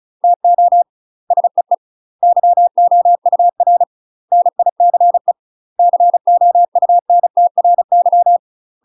Morse code, heard as a keyed tone, about 34 wpm, 700 Hz. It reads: TO SEE YOUR NICE COUNTRY
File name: UK MASTER QSOs TRAINING/35WPM